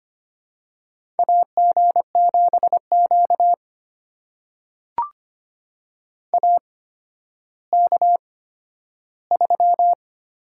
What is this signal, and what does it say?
Morse code, heard as a keyed tone, about 25 wpm, spaced as for 12 wpm, 700 Hz. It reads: AG7Q AK3
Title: USA Call Signs - 4 Characters - 8x Character Spacing - 25wpm